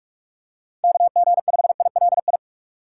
Morse code, heard as a keyed tone, about 44 wpm, 700 Hz. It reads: KG5ILI